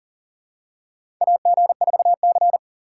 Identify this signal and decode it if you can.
Morse code, heard as a keyed tone, about 40 wpm, 700 Hz. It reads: AG4C
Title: USA Call Signs - 4 Characters - 2x Character Spacing - 40wpm